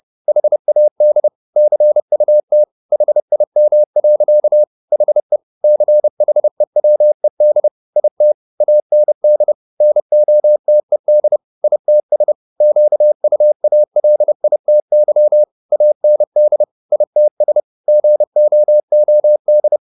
Morse code, heard as a keyed tone, about 30 wpm, 600 Hz. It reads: HAD CUT HIM. HE CHEWED IT AND NOTED ITS QUALITY AND ITS GOOD